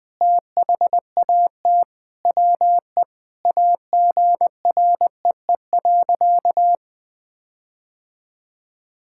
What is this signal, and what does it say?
Morse code, heard as a keyed tone, about 20 wpm, 700 Hz. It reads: THAT WE AGREE.
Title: Sentences with 8-Word Phrases - 20wpm